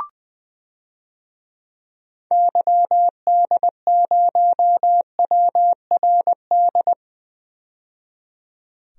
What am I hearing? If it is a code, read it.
Morse code, heard as a keyed tone, about 20 wpm, 700 Hz. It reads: YD0WRD
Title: DXCC-100 Call Signs - 6 Characters - 20wpm